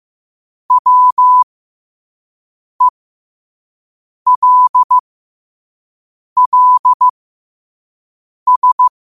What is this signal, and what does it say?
Morse code, heard as a keyed tone, about 15 wpm, spaced as for 5 wpm, 1000 Hz. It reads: WELLS
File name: qotdfiles / 5wpm